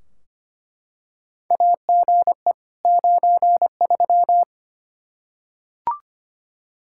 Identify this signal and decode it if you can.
Morse code, heard as a keyed tone, about 25 wpm, 700 Hz. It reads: AGE 93
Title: K1USN SST QSO Practice 25wpm